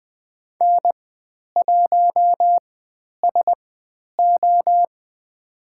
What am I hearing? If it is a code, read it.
Morse code, heard as a keyed tone, about 20 wpm, spaced as for 10 wpm, 700 Hz. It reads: N1SO
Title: 095 Lesson 16g - Letter c - Call Signs 20wpm